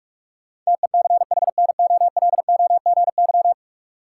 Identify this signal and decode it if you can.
Morse code, heard as a keyed tone, about 46 wpm, 700 Hz. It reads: TECHNOLOGY
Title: Sentences from Top 1200 Words - New Word Drill Alt - 45wpm